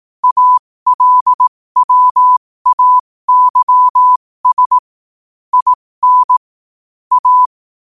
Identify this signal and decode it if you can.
Morse code, heard as a keyed tone, about 18 wpm, spaced as for 15 wpm, 1000 Hz. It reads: ALWAYS IN A
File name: qotdfiles / 15wpm